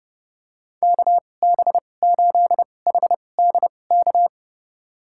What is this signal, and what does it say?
Morse code, heard as a keyed tone, about 32 wpm, spaced as for 22 wpm, 700 Hz. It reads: KB8HDK
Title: USA Call Signs - 6 Characters - 2x Character Spacing - 30wpm